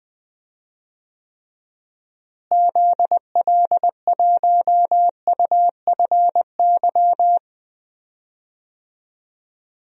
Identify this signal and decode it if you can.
Morse code, heard as a keyed tone, about 20 wpm, 700 Hz. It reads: ZL1UFY
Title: DXCC-100 Call Signs - 6 Characters - Repeated 2x - 20wpm